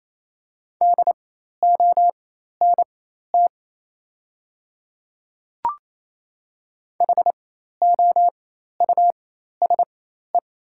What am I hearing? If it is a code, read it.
Morse code, heard as a keyed tone, about 28 wpm, spaced as for 13 wpm, 700 Hz. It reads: DONT HOUSE
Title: 4x Character Spacing - Top 500 Words - 28wpm